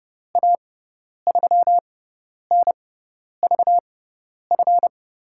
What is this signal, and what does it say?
Morse code, heard as a keyed tone, about 30 wpm, spaced as for 10 wpm, 700 Hz. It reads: A3NVF
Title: USA Call Signs - 6 Characters - 6x Character Spacing - 30wpm